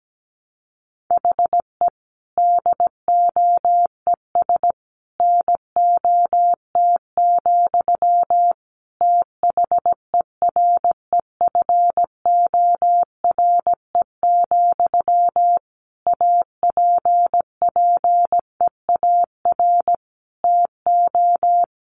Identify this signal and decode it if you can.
Morse code, heard as a keyed tone, about 17 wpm, 700 Hz. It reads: HE DOES NOT, THEREFORE, APPEAR TO
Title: Sentences from Top 1200 Words 17wpm